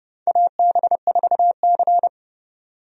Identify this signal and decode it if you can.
Morse code, heard as a keyed tone, about 30 wpm, 700 Hz. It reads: AB4C
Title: USA Call Signs - 4 Characters - 30wpm